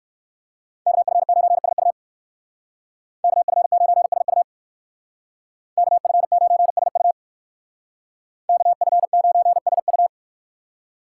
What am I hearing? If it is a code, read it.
Morse code, heard as a keyed tone, about 58 wpm, 700 Hz. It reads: KF9SU KF9SU KF9SU KF9SU